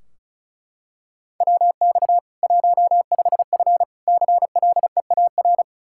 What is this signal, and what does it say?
Morse code, heard as a keyed tone, about 36 wpm, 700 Hz. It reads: WX 15F CLEAR
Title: Practice QSO - Exchange 2 - Info - 35wpm